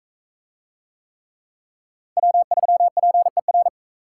Morse code, heard as a keyed tone, about 43 wpm, 700 Hz. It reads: W3PER